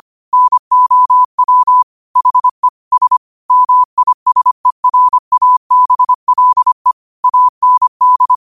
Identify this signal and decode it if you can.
Morse code, heard as a keyed tone, about 25 wpm, 1000 Hz. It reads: NOW HE S MISERABLE AND